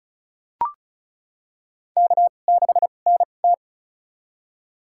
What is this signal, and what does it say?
Morse code, heard as a keyed tone, about 36 wpm, spaced as for 25 wpm, 700 Hz. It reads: K6NT
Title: USA Call Signs - 4 Characters - 2x Character Spacing - 35wpm